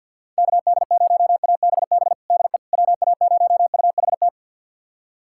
Morse code, heard as a keyed tone, about 52 wpm, 700 Hz. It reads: KD0ABB DE WA0UHT
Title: USA Callsign DE Callsign - 6 Characters - 50wpm